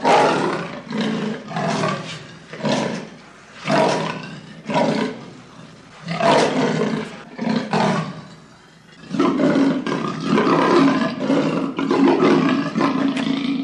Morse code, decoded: none